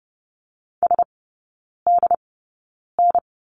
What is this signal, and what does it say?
Morse code, heard as a keyed tone, about 30 wpm, spaced as for 9 wpm, 700 Hz. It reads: SDN
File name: 081 Lesson 14e - Number 2 - Three Characters 30wpm